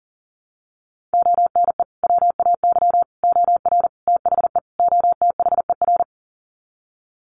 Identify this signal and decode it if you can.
Morse code, heard as a keyed tone, about 40 wpm, 700 Hz. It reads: ONE WAY OR THE OTHER
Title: Sets of 5 Words 40wpm